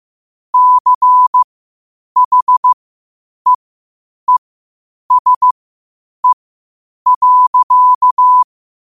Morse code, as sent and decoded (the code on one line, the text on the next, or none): -.-. .... . ... .-.-.-
CHEESE.